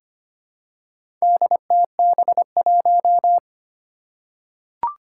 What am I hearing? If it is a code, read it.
Morse code, heard as a keyed tone, about 25 wpm, 700 Hz. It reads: DTB1
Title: Four Challenging Letter-Number Combinations 25wpm